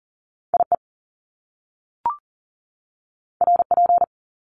Morse code, heard as a keyed tone, about 40 wpm, 700 Hz. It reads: IE RP
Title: Top Two Letter Combinations in Words 40wpm